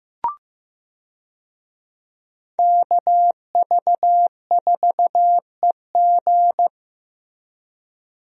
Morse code, heard as a keyed tone, about 15 wpm, 700 Hz. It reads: KV4EG
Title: USA Call Signs - 5 Characters - Repeated 3x - 15wpm